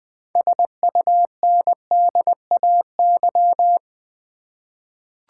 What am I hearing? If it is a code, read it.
Morse code, heard as a keyed tone, about 20 wpm, 700 Hz. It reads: SUNDAY
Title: 6 Letter Words - First 3 Letters Given - 20wpm